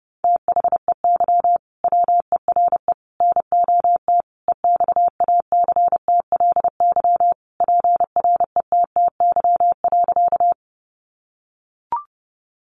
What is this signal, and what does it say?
Morse code, heard as a keyed tone, about 30 wpm, 700 Hz. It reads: THEY WERE NOT EXACTLY PRETTY.
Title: Sentences from Top 1000 Words 30wpm